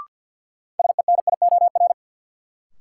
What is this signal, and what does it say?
Morse code, heard as a keyed tone, about 49 wpm, 700 Hz. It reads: SENIOR
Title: Sets of 5 Words Encore - New Word Drill - 50wpm